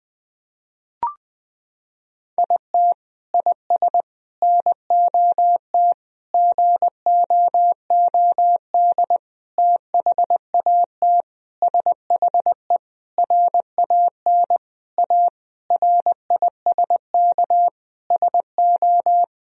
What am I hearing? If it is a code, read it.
Morse code, heard as a keyed tone, about 20 wpm, 700 Hz. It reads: IT IS NOT GOOD THAT SHE RAN A RISK SO
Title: Sentences from Top 1000 Words - Review Part 4 - 20wpm